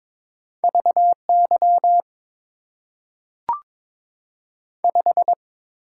Morse code, heard as a keyed tone, about 22 wpm, 700 Hz. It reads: V Y 5
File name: Top 90 QSO Elements - 8x Character Spacing - 22wpm